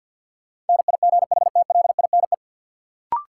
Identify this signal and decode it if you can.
Morse code, heard as a keyed tone, about 54 wpm, 700 Hz. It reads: NIGHTLINE
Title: Sets of 5 Words - New Word Drill - 50wpm